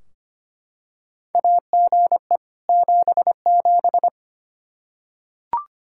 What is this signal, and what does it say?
Morse code, heard as a keyed tone, about 25 wpm, 700 Hz. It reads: AGE 77
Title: Practice QSO - Exchange 3 - More Info - 25wpm